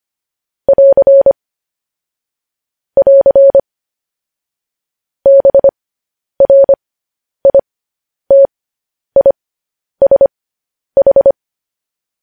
Morse code, heard as a keyed tone, about 25 wpm, spaced as for 10 wpm, 550 Hz. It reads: <AR> <AR> BRITISH
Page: news headlines in morse code at 10 wpm